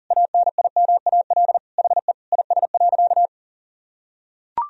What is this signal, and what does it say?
Morse code, heard as a keyed tone, about 40 wpm, 700 Hz. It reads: ANIMAL HE IS.
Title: Sentences from Top 1000 Words - Review Part 1 - 40wpm